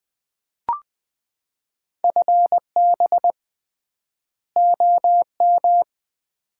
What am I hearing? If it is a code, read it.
Morse code, heard as a keyed tone, about 20 wpm, 700 Hz. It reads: FB OM
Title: Common QSO Phrases - 3x Word-Spacing - 20wpm